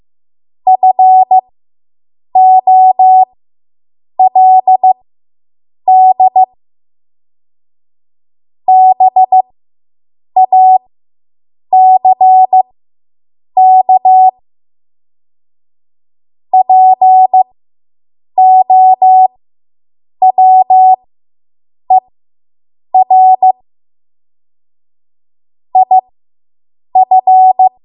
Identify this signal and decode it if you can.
Morse code, heard as a keyed tone, about 15 wpm, spaced as for 7 wpm, 750 Hz. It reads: FOLD BACK POWER IF